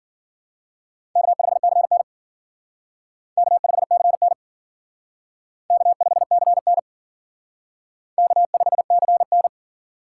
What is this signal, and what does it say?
Morse code, heard as a keyed tone, about 51 wpm, 700 Hz. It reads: K5CN K5CN K5CN K5CN